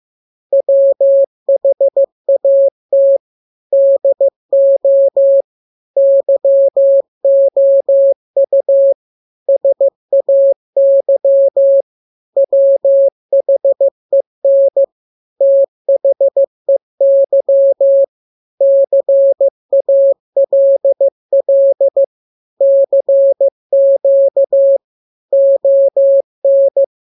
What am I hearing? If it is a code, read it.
Morse code, heard as a keyed tone, about 15 wpm, 550 Hz. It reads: WHAT DO YOU SAY WHEN THEY CALL CQ ON
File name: jokes / joke 21 15 0000